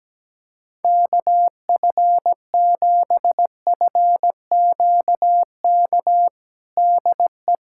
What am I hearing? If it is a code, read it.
Morse code, heard as a keyed tone, about 17 wpm, 700 Hz. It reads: KF7FQK DE
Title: USA Callsign DE Callsign - 6 Characters - 17wpm